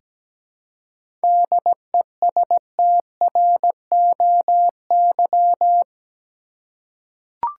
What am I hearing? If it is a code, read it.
Morse code, heard as a keyed tone, about 17 wpm, 700 Hz. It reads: DESTROY
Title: Seven Letter Words - 17wpm